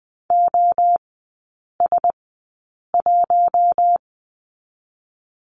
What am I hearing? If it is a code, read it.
Morse code, heard as a keyed tone, about 20 wpm, spaced as for 8 wpm, 700 Hz. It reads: OS1
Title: 060 Lesson 11e - Letter h - Three Characters 20wpm